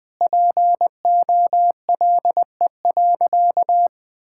Morse code, heard as a keyed tone, about 20 wpm, 700 Hz. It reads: POLE.
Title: Sentences with 5-Word Phrases - Spoken Beginning - 20wpm